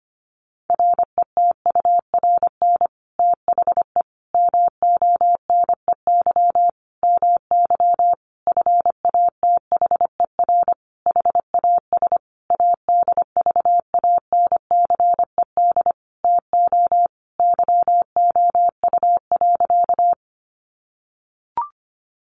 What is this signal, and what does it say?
Morse code, heard as a keyed tone, about 25 wpm, 700 Hz. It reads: RETURN THE MONEY MY FATHER HAS ADVANCED TO YOU.
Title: Sentences from Top 1200 Words 25wpm